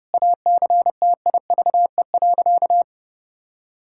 Morse code, heard as a keyed tone, about 32 wpm, 700 Hz. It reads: ACTIVE.
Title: Sentences from Top 1000 Words - Review Part 6 - 30wpm